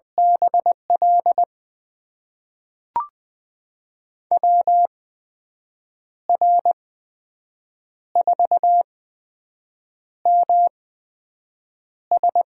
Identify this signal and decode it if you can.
Morse code, heard as a keyed tone, about 20 wpm, spaced as for 6 wpm, 700 Hz. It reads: BL WR4MS
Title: USA Call Signs - 5 Characters - 8x Character Spacing - 20wpm